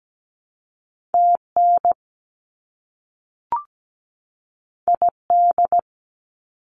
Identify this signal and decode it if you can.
Morse code, heard as a keyed tone, about 17 wpm, 700 Hz. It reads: TN ID
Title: US States 17wpm